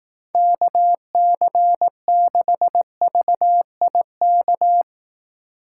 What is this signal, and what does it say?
Morse code, heard as a keyed tone, about 19 wpm, 700 Hz. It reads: KC6VIK
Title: Speed Racing - USA 2x3 Call Signs 15wpm